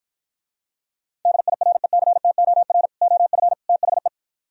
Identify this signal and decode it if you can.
Morse code, heard as a keyed tone, about 52 wpm, 700 Hz. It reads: DIRECTOR OF THE